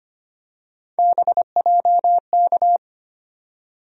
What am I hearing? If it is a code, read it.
Morse code, heard as a keyed tone, about 25 wpm, 700 Hz. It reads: BJK
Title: USA Call Sign Postfixes - 3 Letters 25wpm